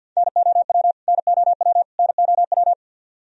Morse code, heard as a keyed tone, about 52 wpm, 700 Hz. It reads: NOW NOW NOW
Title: Top 100 Words Repeated 50wpm